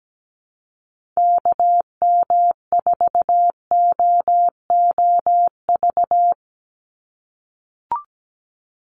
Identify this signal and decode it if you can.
Morse code, heard as a keyed tone, about 17 wpm, 700 Hz. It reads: KM4OOV
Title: USA Call Signs 17wpm